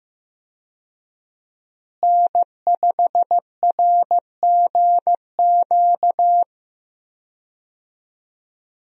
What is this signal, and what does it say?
Morse code, heard as a keyed tone, about 15 wpm, 700 Hz. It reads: N5RGQ
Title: USA Call Signs - 5 Characters - Repeated 3x - 15wpm